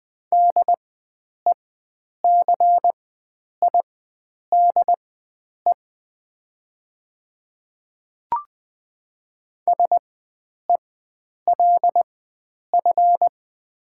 Morse code, heard as a keyed tone, about 20 wpm, spaced as for 9 wpm, 700 Hz. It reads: DECIDE SELF
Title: 4x Character Spacing - Top 500 Words - 20wpm